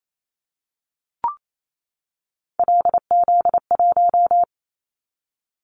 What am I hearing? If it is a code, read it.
Morse code, heard as a keyed tone, about 28 wpm, 700 Hz. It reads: LZ1
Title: Callsign Prefixes for Top 50 DXCC - 28wpm